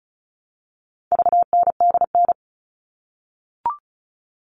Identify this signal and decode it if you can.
Morse code, heard as a keyed tone, about 34 wpm, 700 Hz. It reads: VNDN
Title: All Cut Numbers - 4-Digit 35wpm